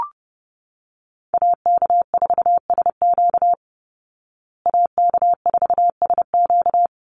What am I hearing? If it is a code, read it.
Morse code, heard as a keyed tone, about 30 wpm, 700 Hz. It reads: AK4SQ AK4SQ
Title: Call Signs Repeated 3x - 30wpm